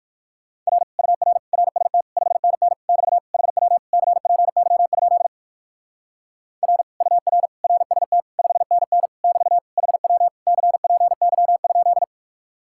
Morse code, heard as a keyed tone, about 59 wpm, 700 Hz. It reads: R UR RST 5NN <BT> HW CPY? R UR RST 5NN <BT> HW CPY?